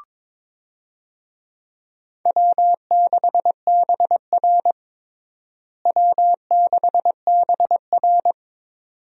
Morse code, heard as a keyed tone, about 22 wpm, 700 Hz. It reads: W6BR W6BR
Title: USA Call Signs - 4 Characters - Repeated 2x - 22wpm